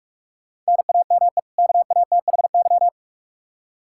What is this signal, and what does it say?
Morse code, heard as a keyed tone, about 45 wpm, 700 Hz. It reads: NAME KATHY
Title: Practice QSO - Exchange 1 - Beginning - 45wpm